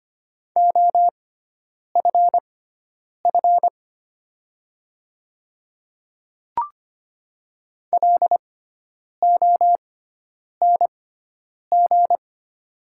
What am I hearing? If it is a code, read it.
Morse code, heard as a keyed tone, about 25 wpm, spaced as for 9 wpm, 700 Hz. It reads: OFF LONG
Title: 6x Character Spacing - Top 300 Words - 25wpm